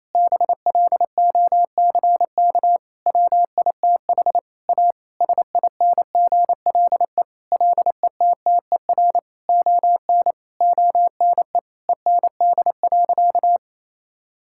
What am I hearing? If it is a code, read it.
Morse code, heard as a keyed tone, about 28 wpm, 700 Hz. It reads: BLOCK WITH A SINGLE LETTER ON ONE END.
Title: Sentences from Top 1600 Words 28wpm